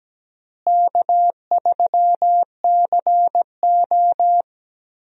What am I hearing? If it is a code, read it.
Morse code, heard as a keyed tone, about 17 wpm, 700 Hz. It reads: K3CO
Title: USA Call Signs - 4 Characters - Repeated 2x - 17wpm